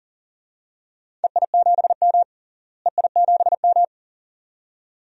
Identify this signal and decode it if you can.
Morse code, heard as a keyed tone, about 40 wpm, 700 Hz. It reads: EI7M EI7M
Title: DXCC-100 Call Signs - 4 Characters - Repeated 2x - 40wpm